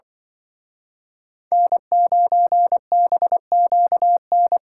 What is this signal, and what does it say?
Morse code, heard as a keyed tone, about 24 wpm, 700 Hz. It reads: N9BQN
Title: Speed Racing - USA 1x3 Call Signs 20wpm